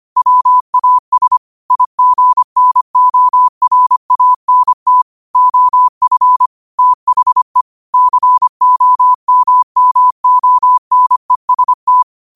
Morse code, heard as a keyed tone, about 25 wpm, 1000 Hz. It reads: WAS IGNORANT OF THE COMMONEST
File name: qotdfiles / 25wpm